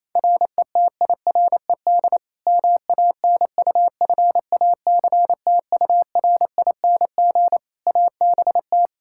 Morse code, heard as a keyed tone, about 28 wpm, 700 Hz. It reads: RETIRED MANUFACTURING ABT
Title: Practice QSO - Exchange 3 - More Info - 28wpm